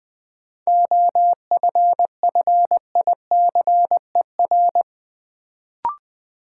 Morse code, heard as a keyed tone, about 21 wpm, 700 Hz. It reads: OFFICER
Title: Sentences from Top 1200 Words - New Word Drill - 20wpm